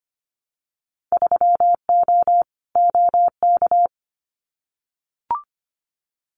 Morse code, heard as a keyed tone, about 25 wpm, 700 Hz. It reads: 3O OK
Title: Winter Field Day Exchanges 25wpm